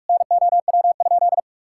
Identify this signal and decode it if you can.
Morse code, heard as a keyed tone, about 46 wpm, 700 Hz. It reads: NOW?
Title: Sentences from Top 1000 Words - Review Part 5 - 45wpm